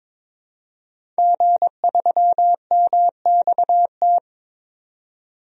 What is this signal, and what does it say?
Morse code, heard as a keyed tone, about 22 wpm, 700 Hz. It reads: G3MXT
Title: DXCC-100 Call Signs - 5 Characters - 22wpm